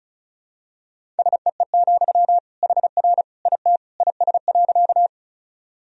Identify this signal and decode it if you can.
Morse code, heard as a keyed tone, about 36 wpm, 700 Hz. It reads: SEE, HR IT IS.
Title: Sentences from Top 200 Words and Abbreviations 35wpm